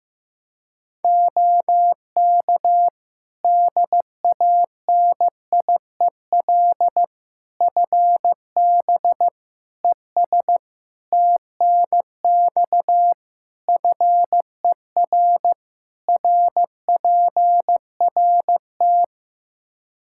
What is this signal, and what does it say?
Morse code, heard as a keyed tone, about 16 wpm, 700 Hz. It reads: OK DANIEL FB ES TNX FER RPRT
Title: Practice QSO - Exchange 2 - Info - 15wpm